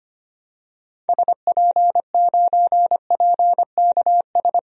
Morse code, heard as a keyed tone, about 25 wpm, 700 Hz. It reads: SP9PKS
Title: DXCC-100 Call Signs - 6 Characters - Repeated 2x - 25wpm